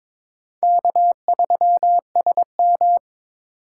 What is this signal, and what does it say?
Morse code, heard as a keyed tone, about 22 wpm, 700 Hz. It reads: K3SM
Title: USA Call Signs - 2x Character Spacing - 22wpm